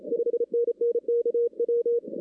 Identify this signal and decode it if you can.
Morse code, heard as a keyed tone, about 29 wpm, 440 Hz. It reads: 5NN KW
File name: 2016-ARRL-DX-CW-K5TR